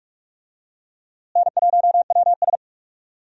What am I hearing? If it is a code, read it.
Morse code, heard as a keyed tone, about 46 wpm, 700 Hz. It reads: N1WS
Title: USA Call Signs - 4 Characters - Repeated 3x - 45wpm